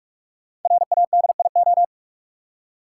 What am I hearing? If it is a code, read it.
Morse code, heard as a keyed tone, about 44 wpm, 700 Hz. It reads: RADIO
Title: Five Letter Words - 45wpm